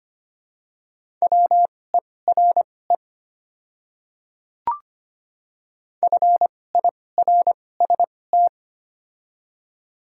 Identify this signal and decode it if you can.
Morse code, heard as a keyed tone, about 25 wpm, spaced as for 18 wpm, 700 Hz. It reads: WERE FIRST
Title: 2x Character Spacing - Top 100 Words - 25wpm